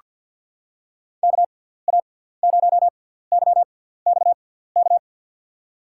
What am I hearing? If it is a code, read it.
Morse code, heard as a keyed tone, about 47 wpm, spaced as for 17 wpm, 700 Hz. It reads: KA0YXK
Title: USA Call Signs - 6 Characters - 6x Character Spacing - 50wpm